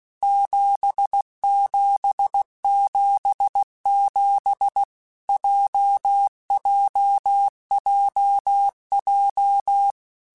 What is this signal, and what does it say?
Morse code, heard as a keyed tone, about 16 wpm, 800 Hz. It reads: 7777 JJJJ
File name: Basic Letter Groups By7Groups / JZ17 / JZ17@15